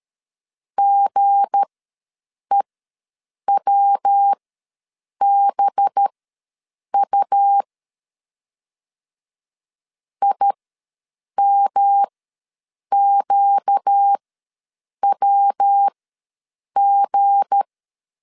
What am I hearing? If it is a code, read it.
Morse code, heard as a keyed tone, about 13 wpm, spaced as for 7 wpm, 800 Hz. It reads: GEWBU IMQWG
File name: LEZIONI / 7 GF9Z / 7 G ws 12wpm 3140